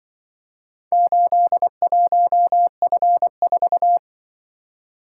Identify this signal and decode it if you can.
Morse code, heard as a keyed tone, about 25 wpm, 700 Hz. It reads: 81F4